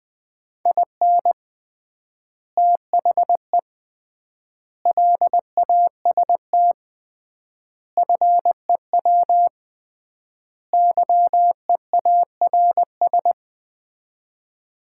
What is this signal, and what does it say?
Morse code, heard as a keyed tone, about 21 wpm, 700 Hz. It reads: IN THE LAST FEW YEARS
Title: 3x Word-Spacing - Sets of 5 Words - 20wpm